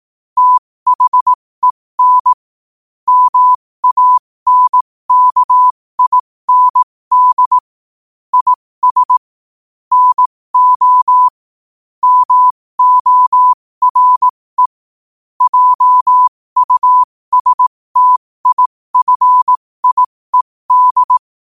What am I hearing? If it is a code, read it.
Morse code, heard as a keyed tone, about 18 wpm, spaced as for 15 wpm, 1000 Hz. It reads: THEN MANKIND IS NO MORE JUSTIFIED
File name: qotdfiles / 15wpm